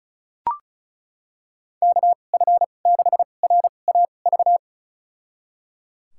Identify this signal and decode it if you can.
Morse code, heard as a keyed tone, about 36 wpm, spaced as for 25 wpm, 700 Hz. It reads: KF6RAV
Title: USA Call Signs - 6 Characters - 2x Character Spacing - 35wpm